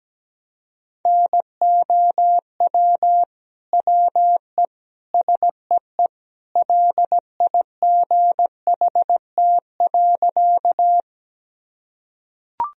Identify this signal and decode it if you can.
Morse code, heard as a keyed tone, about 17 wpm, 700 Hz. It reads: NOW WE SEE LIGHT.
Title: Sentences from Top 1000 Words - Review Part 1 - 17wpm